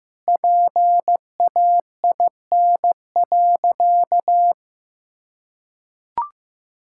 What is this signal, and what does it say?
Morse code, heard as a keyed tone, about 15 wpm, 700 Hz. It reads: PAIN.
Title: Sentences from Top 1600 Words 15wpm